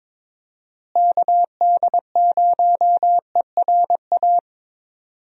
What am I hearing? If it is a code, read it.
Morse code, heard as a keyed tone, about 22 wpm, 700 Hz. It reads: KD0ERA
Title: USA Call Signs - 6 Characters - Repeated 2x - 22wpm